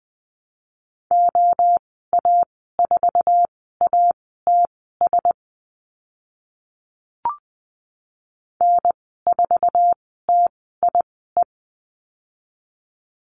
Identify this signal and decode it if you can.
Morse code, heard as a keyed tone, about 20 wpm, spaced as for 14 wpm, 700 Hz. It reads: OA4ATS N4TIE
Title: 076 Lesson 13g - Letter L - Call Signs 20wpm